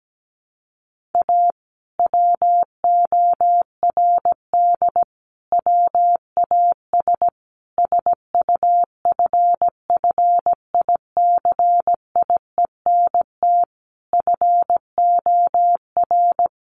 Morse code, heard as a keyed tone, about 17 wpm, 700 Hz. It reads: A WORD WAS SUFFICIENT FOR
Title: Sentences from Top 1400 Words 17wpm